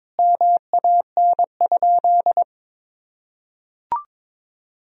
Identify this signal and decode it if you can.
Morse code, heard as a keyed tone, about 23 wpm, 700 Hz. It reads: MAN?
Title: Sentences from Top 1000 Words - Review Part 4 - 22wpm